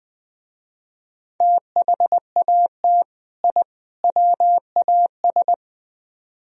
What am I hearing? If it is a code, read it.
Morse code, heard as a keyed tone, about 20 wpm, 700 Hz. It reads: THAT I WAS
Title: Speed-Racing - Sets of 3 Words 17wpm